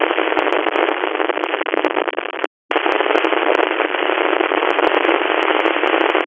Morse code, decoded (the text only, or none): none